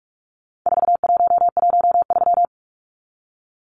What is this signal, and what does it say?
Morse code, heard as a keyed tone, about 44 wpm, 700 Hz. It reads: 4113
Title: Numbers - 4-Digit 45wpm